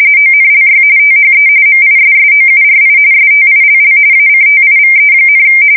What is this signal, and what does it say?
RTTY, about 45.45 baud. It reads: ONLY THE MISSOURI STILL ACTIVE,
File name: fsk170 / battleshipnj